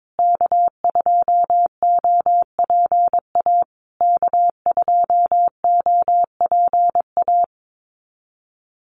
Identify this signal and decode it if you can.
Morse code, heard as a keyed tone, about 22 wpm, 700 Hz. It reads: K2OPA K2OPA
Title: USA Call Signs Repeated 3x 22wpm